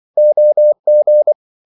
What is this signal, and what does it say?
Morse code, heard as a keyed tone, about 24 wpm, 600 Hz. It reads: OG
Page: Ferie 24 wpm